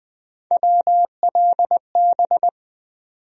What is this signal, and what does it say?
Morse code, heard as a keyed tone, about 20 wpm, 700 Hz. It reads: WLB
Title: USA Call Sign Postfixes - 3 Letters 20wpm